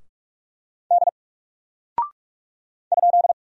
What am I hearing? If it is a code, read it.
Morse code, heard as a keyed tone, about 44 wpm, 700 Hz. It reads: D ?
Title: Single Character (Letter:Number:Punctuation:Prosign) Alt 45wpm